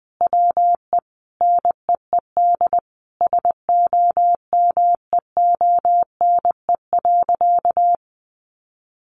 Morse code, heard as a keyed tone, about 20 wpm, 700 Hz. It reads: WE NEED SOMEONE.
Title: Sentences from Top 1300 Words 20wpm